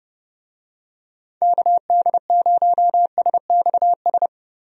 Morse code, heard as a keyed tone, about 30 wpm, 700 Hz. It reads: KD0SXS
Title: USA Call Signs - 6 Characters - 30wpm